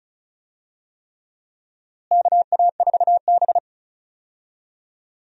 Morse code, heard as a keyed tone, about 36 wpm, 700 Hz. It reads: KA4B
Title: USA Call Signs - 4 Characters - Repeated 3x - 35wpm